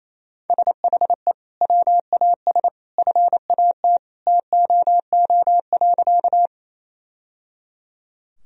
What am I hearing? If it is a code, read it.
Morse code, heard as a keyed tone, about 27 wpm, 700 Hz. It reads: SHE WAS FAT TOO.
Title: Sentences from Top 1000 Words - Review Part 4 - 28wpm